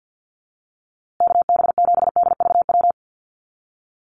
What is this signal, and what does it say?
Morse code, heard as a keyed tone, about 48 wpm, 700 Hz. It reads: KB7DVW
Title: USA Call Signs Repeated 3x 50wpm